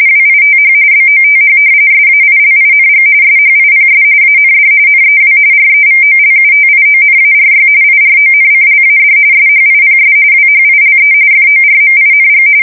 RTTY, about 45.45 baud. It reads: WITH BOTH SAILING SHIPS AND NEW REVOLUTIONARY IRONCLAD SHIPS WHILE FORMING
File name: fsk170 / usnavy1